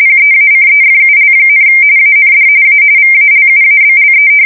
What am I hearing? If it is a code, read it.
RTTY, about 45.45 baud. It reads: PEACEFUL, AND A VALUABLE
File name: fsk170 / usnavy3